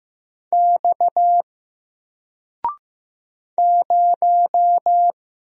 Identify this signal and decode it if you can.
Morse code, heard as a keyed tone, about 15 wpm, 700 Hz. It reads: X 0
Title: Single Letter-Number 15wpm